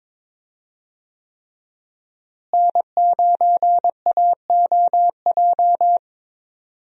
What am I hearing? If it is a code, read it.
Morse code, heard as a keyed tone, about 22 wpm, 700 Hz. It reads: N9AOJ
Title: USA Call Signs - 5 Characters - Repeated 3x - 22wpm